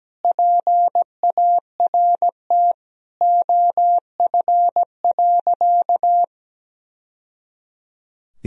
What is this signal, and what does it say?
Morse code, heard as a keyed tone, about 17 wpm, 700 Hz. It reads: PART OF.
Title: Sentences from Top 1000 Words - Review Part 1 - 17wpm